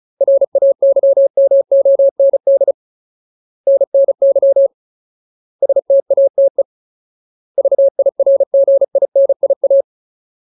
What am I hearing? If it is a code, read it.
Morse code, heard as a keyed tone, about 34 wpm, 550 Hz. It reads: RAYMOND NNY STATE VIRGINIA
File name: Ultimate Word List / 35WPM Ultimate